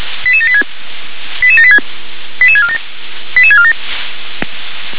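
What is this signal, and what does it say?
Five-tone selective call (ZVEI), heard as quick runs of five tones, tones of about 70 ms. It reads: 88786 88786 88647 88647
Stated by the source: ZVEI five-tone selective call 88786 received with an SDR.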